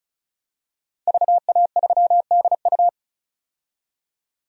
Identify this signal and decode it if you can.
Morse code, heard as a keyed tone, about 34 wpm, 700 Hz. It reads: VA3DU
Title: DXCC-100 Call Signs - 5 Characters - 35wpm